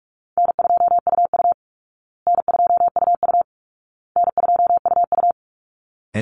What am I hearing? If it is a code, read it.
Morse code, heard as a keyed tone, about 44 wpm, 700 Hz. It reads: N2UU N2UU N2UU
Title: Call Signs Repeated 2x - 45wpm